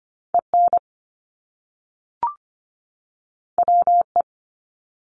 Morse code, heard as a keyed tone, about 25 wpm, 700 Hz. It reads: EN WE
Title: Top Two Letter Combinations in Words 25wpm